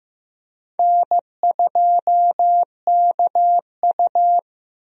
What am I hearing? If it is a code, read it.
Morse code, heard as a keyed tone, about 15 wpm, 700 Hz. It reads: N2KU
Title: USA Call Signs - 4 Characters - Repeated 2x - 15wpm